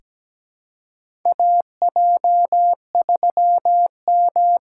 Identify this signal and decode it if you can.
Morse code, heard as a keyed tone, about 17 wpm, 700 Hz. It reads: AJ3M
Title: USA Call Signs - 3x Character Spacing - 17wpm